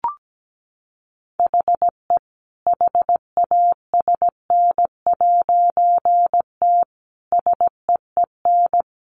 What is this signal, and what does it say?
Morse code, heard as a keyed tone, about 17 wpm, 700 Hz. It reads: HE HASN'T SEEN
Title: A Sound Of Thunder 17wpm